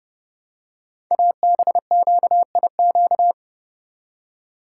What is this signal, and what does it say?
Morse code, heard as a keyed tone, about 30 wpm, 690 Hz. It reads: ABQIQ
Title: Code Groups - 5 Letters - 30wpm